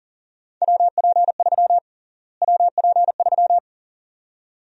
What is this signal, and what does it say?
Morse code, heard as a keyed tone, about 40 wpm, 700 Hz. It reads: WP3 WP3
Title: USA Call Sign Prefixes - Repeated 2x - 40wpm